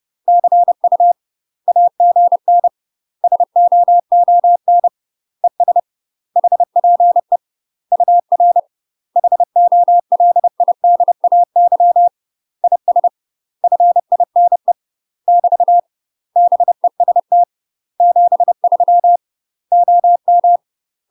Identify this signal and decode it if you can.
Morse code, heard as a keyed tone, about 30 wpm, 700 Hz. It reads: CU AGN SOON ES HPE UR HOLIDAY IS FINE <BT> BEST 73 OM
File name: UK MASTER QSOs TRAINING / 30WPM / UK MASTER QSOs 0039